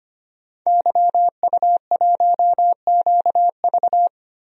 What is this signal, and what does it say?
Morse code, heard as a keyed tone, about 25 wpm, 700 Hz. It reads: YU1QV
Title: DXCC-100 Call Signs - 5 Characters - Repeated 2x - 25wpm